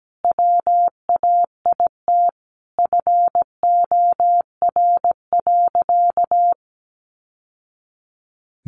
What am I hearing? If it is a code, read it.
Morse code, heard as a keyed tone, about 17 wpm, 700 Hz. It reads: WAIT FOR.
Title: Sentences from Top 800 Words 17wpm